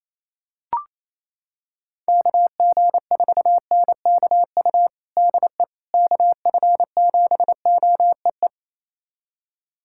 Morse code, heard as a keyed tone, about 28 wpm, 700 Hz. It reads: KG4NKU DE KF7OEE
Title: USA Callsign DE Callsign - 6 Characters - 28wpm